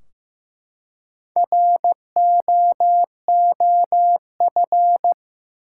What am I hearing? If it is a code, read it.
Morse code, heard as a keyed tone, about 15 wpm, 700 Hz. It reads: ROOF
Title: Four Letter Words - 15wpm